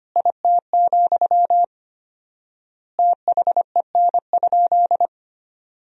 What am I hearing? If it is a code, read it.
Morse code, heard as a keyed tone, about 25 wpm, 700 Hz. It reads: IT, THEN?
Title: 4x Word-Spacing - Sentences from Top 200 Words - 25wpm